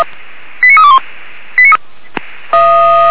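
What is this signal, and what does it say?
Five-tone selective call (ZVEI), heard as quick runs of five tones, tones of about 70 ms. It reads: 89321
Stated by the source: ZVEI five-tone call received on SDR.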